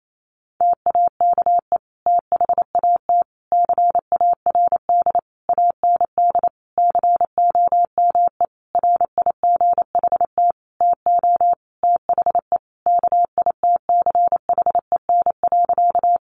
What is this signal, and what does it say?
Morse code, heard as a keyed tone, about 28 wpm, 700 Hz. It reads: TAKE THAT CARD AND COME RIGHT TO THE KITCHEN.
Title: Sentences from Top 1400 Words 28wpm